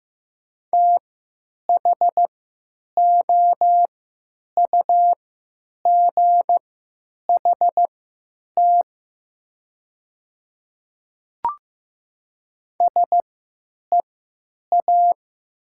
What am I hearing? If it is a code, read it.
Morse code, heard as a keyed tone, about 15 wpm, spaced as for 9 wpm, 700 Hz. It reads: THOUGHT SEA
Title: 3x Character Spacing - Top 500 Words - 15wpm